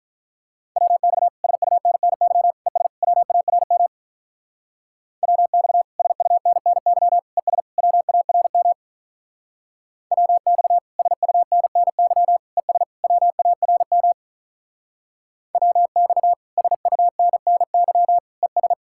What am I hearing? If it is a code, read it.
Morse code, heard as a keyed tone, about 44 wpm, 700 Hz. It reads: WX SUNNY ES WARM WX SUNNY ES WARM WX SUNNY ES WARM WX SUNNY ES